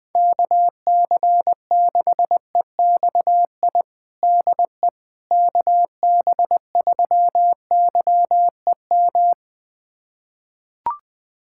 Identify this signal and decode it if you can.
Morse code, heard as a keyed tone, about 21 wpm, 700 Hz. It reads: KC6EXI DE KB3YEM
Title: USA Callsign DE Callsign 20wpm